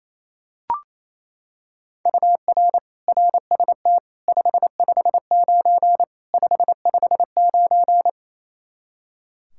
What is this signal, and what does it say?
Morse code, heard as a keyed tone, about 28 wpm, 700 Hz. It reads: UR RST 559 559
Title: Practice QSO - Exchange 1 - Beginning - 28wpm